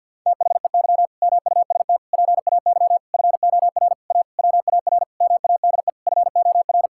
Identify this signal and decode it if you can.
Morse code, heard as a keyed tone, about 50 wpm, 700 Hz. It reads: THEY MUST PAY FOR A WAR MADE FOR
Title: Sentences from Top 1000 Words - Review Part 6 - 50wpm